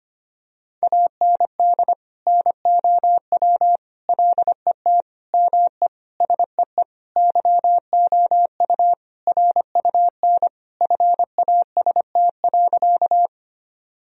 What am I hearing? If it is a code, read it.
Morse code, heard as a keyed tone, about 25 wpm, 700 Hz. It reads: AND NOW LET ME SEE YOU RUN FAST.
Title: Sentences from Top 1000 Words - Review Part 2 - 25wpm